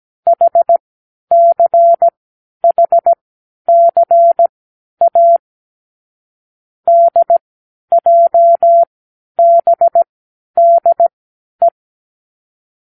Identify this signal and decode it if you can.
Morse code, heard as a keyed tone, about 17 wpm, spaced as for 10 wpm, 680 Hz. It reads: HCHCA DJBDE
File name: k7qo / Lesson18(A-L)